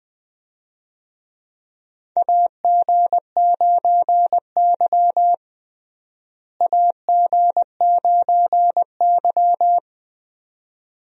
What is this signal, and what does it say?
Morse code, heard as a keyed tone, about 20 wpm, 700 Hz. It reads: AG9Y AG9Y
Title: USA Call Signs - 4 Characters - Repeated 2x - 20wpm